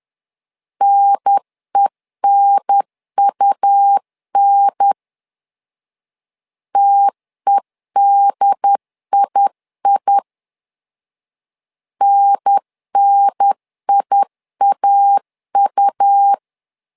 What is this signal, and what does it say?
Morse code, heard as a keyed tone, about 10 wpm, 800 Hz. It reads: NENUN TEDII NNIAU
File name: LEZIONI / 2 UHDN / 2 UHDN 10wpm 1904